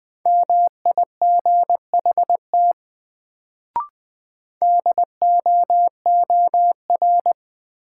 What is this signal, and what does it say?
Morse code, heard as a keyed tone, about 20 wpm, 700 Hz. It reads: MIGHT DOOR
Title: Sentences from Top 300 Words - New Word Drill - 20wpm